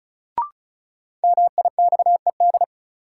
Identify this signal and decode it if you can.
Morse code, heard as a keyed tone, about 34 wpm, 700 Hz. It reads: MIXED
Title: Sentences from Top 1500 Words - New Word Drill - 35wpm